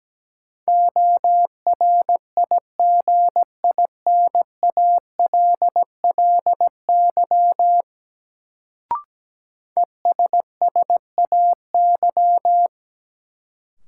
Morse code, heard as a keyed tone, about 17 wpm, 700 Hz. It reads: ORIGINALLY ESSAY
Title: Sentences from Top 1500 Words - New Word Drill - 17wpm